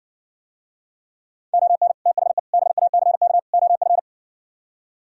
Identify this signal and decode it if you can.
Morse code, heard as a keyed tone, about 60 wpm, 700 Hz. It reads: ON THE BACK OF